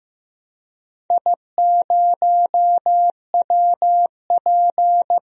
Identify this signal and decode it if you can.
Morse code, heard as a keyed tone, about 15 wpm, 700 Hz. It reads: I0WP
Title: DXCC-100 Call Signs - 4 Characters - Repeated 3x - 15wpm